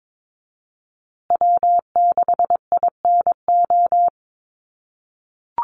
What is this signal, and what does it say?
Morse code, heard as a keyed tone, about 22 wpm, 700 Hz. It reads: W6INO
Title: USA Call Signs Repeated 2x 22wpm